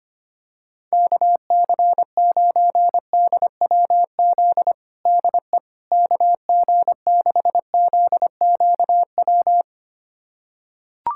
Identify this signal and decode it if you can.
Morse code, heard as a keyed tone, about 25 wpm, 700 Hz. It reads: KC9DWZ DE KG6ZQW
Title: USA Callsign DE Callsign - 6 Characters - 25wpm